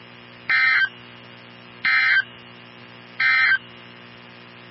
Emergency Alert System (SAME) tones, heard as harsh buzzing bursts. It is heard from 0.4 to 3.7 s.
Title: NOAA Weather Radio 162.550